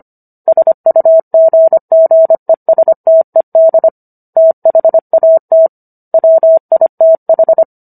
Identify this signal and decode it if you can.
Morse code, heard as a keyed tone, about 25 wpm, 650 Hz. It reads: SUGGESTED THAT WITH